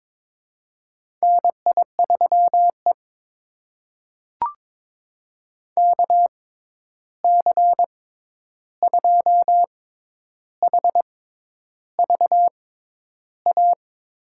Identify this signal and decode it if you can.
Morse code, heard as a keyed tone, about 22 wpm, spaced as for 10 wpm, 700 Hz. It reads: NI3E KC2HVA
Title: USA Call Signs - 6x Character Spacing - 22wpm